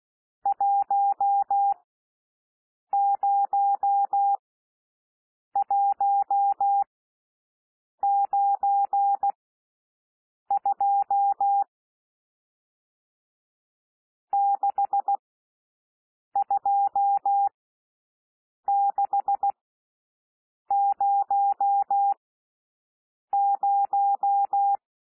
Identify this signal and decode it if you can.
Morse code, heard as a keyed tone, about 16 wpm, spaced as for 6 wpm, 800 Hz. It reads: 10192 62600
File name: cwmp3 / N1 SOUND NUMBERS 16 wpm